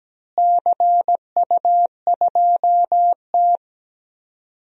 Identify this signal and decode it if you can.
Morse code, heard as a keyed tone, about 17 wpm, 700 Hz. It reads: CU2T
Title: DXCC-100 Call Signs - 4 Characters - Repeated 3x - 17wpm